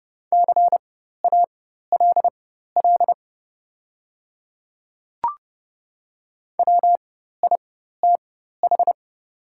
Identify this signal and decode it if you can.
Morse code, heard as a keyed tone, about 29 wpm, spaced as for 14 wpm, 690 Hz. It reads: CALL WITH
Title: 4x Character Spacing - Top 100 Words - 30wpm